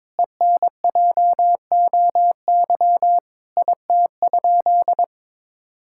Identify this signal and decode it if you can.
Morse code, heard as a keyed tone, about 22 wpm, 700 Hz. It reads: ENJOY IT?
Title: Sentences from Top 1000 Words - Review Part 5 - 22wpm